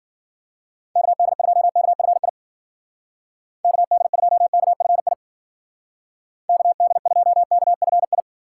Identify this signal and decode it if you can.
Morse code, heard as a keyed tone, about 49 wpm, 700 Hz. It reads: KD2KFI KD2KFI KD2KFI
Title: Speed Racing - USA Call Signs 40wpm